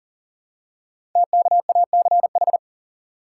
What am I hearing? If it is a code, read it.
Morse code, heard as a keyed tone, about 38 wpm, 690 Hz. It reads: TKACH
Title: Code Groups - 5 Letters - 40wpm